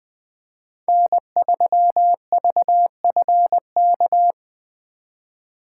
Morse code, heard as a keyed tone, about 24 wpm, 700 Hz. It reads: N3VFK
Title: Speed Racing - USA 1x3 Call Signs 20wpm